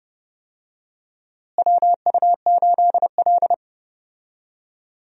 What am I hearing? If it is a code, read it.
Morse code, heard as a keyed tone, about 30 wpm, 700 Hz. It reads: WU8L
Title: USA Call Signs - 4 Characters - Repeated 3x - 30wpm